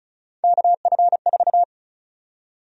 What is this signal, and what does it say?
Morse code, heard as a keyed tone, about 34 wpm, 700 Hz. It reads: KF4
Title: USA Call Sign Prefixes - 35wpm